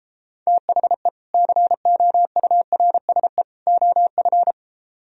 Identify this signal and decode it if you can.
Morse code, heard as a keyed tone, about 33 wpm, 700 Hz. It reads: THE COURSE OF